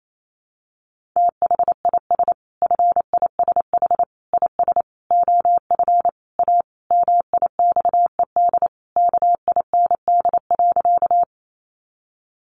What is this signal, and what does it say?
Morse code, heard as a keyed tone, about 28 wpm, 700 Hz. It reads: THIS FISH IS OF A MIXED KIND.
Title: Sentences from Top 1500 Words 28wpm